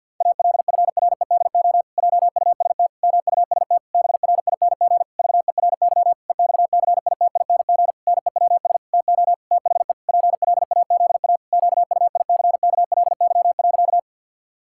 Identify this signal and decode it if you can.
Morse code, heard as a keyed tone, about 50 wpm, 700 Hz. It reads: ALFREDO JUST MUST BRING VERY EXCITING NEWS TO THE PLAZA QUICKLY.
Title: Pangram Challenge v2 - 50wpm